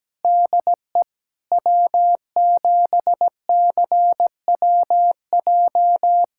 Morse code, heard as a keyed tone, about 17 wpm, 700 Hz. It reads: DE W7CWJ
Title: USA Callsign DE Callsign - 5 Characters - 17wpm